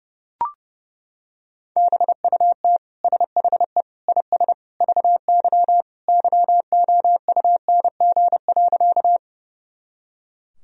Morse code, heard as a keyed tone, about 30 wpm, 700 Hz. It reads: BUT SHE IS VY YOUNG.
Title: Sentences from Top 400 Words and Abbreviations 30wpm